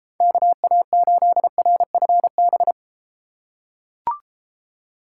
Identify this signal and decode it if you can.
Morse code, heard as a keyed tone, about 33 wpm, 700 Hz. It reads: KA8RFB